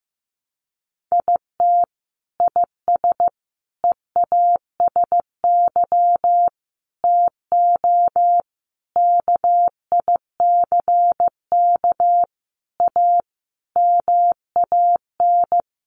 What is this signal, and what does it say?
Morse code, heard as a keyed tone, about 15 wpm, 700 Hz. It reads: IT IS EASY TO KICK A MAN
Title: Sentences from Top 1300 Words 15wpm